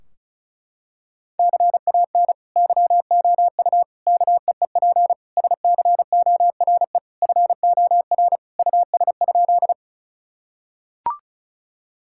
Morse code, heard as a keyed tone, about 36 wpm, 700 Hz. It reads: CAN YOU KEEP SCORE FOR US?
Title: Sentences from Top 1600 Words 35wpm